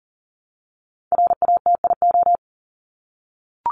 Morse code, heard as a keyed tone, about 40 wpm, 700 Hz. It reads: RATIO
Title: Top 1751-2000 Words 40wpm